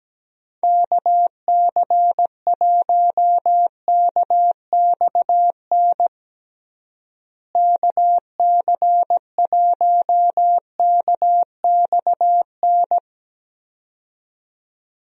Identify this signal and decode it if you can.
Morse code, heard as a keyed tone, about 17 wpm, 700 Hz. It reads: KC1KXN KC1KXN
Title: USA Call Signs - 6 Characters - Repeated 2x - 17wpm